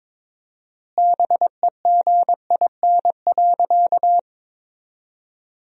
Morse code, heard as a keyed tone, about 22 wpm, 700 Hz. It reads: BEGIN.
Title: 4x Word-Spacing - Sentences from Top 300 Words - 22wpm